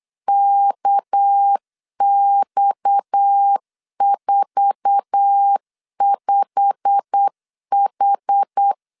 Morse code, heard as a keyed tone, about 8 wpm, 800 Hz. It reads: KX45H
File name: LEZIONI / 8 YJ45 / 8 YJ45 8wpm 3214